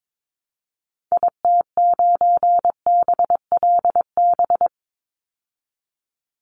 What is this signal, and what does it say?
Morse code, heard as a keyed tone, about 22 wpm, 700 Hz. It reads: IT9BLB
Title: Call Signs Repeated 3x - 22wpm